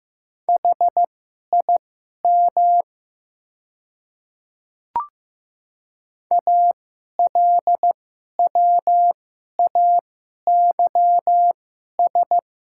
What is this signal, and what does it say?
Morse code, heard as a keyed tone, about 15 wpm, spaced as for 11 wpm, 700 Hz. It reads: HIM ALWAYS
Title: 2x Character Spacing - Top 300 Words - 15wpm